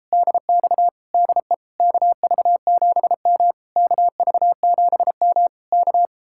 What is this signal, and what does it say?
Morse code, heard as a keyed tone, about 33 wpm, 700 Hz. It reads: DX DE KV7M KV7M K